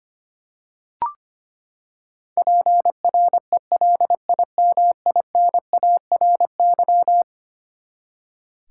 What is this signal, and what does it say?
Morse code, heard as a keyed tone, about 25 wpm, 700 Hz. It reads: PRELIMINARY
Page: morse-code-ninja-practice-25wpm